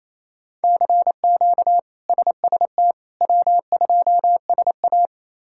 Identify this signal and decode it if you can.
Morse code, heard as a keyed tone, about 28 wpm, 700 Hz. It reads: CQ SST W2SA
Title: K1USN SST QSO Practice - 28wpm